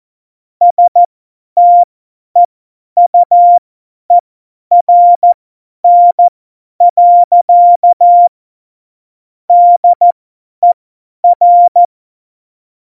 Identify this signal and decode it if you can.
Morse code, heard as a keyed tone, about 14 wpm, spaced as for 10 wpm, 700 Hz. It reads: STEUERN. DER
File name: Morsekurs / Witze / Witz 05. Januar 2021, 50 F70